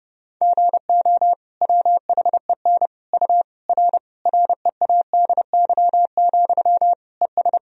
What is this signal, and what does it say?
Morse code, heard as a keyed tone, about 30 wpm, 700 Hz. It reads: GO WHEN U R READY, ES
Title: Sentences from Top 500 Words and Abbreviations 30wpm